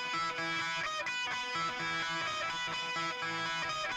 Emergency Alert System (SAME) tones, heard as harsh buzzing bursts. It is heard from 0.4 to 3.5 s.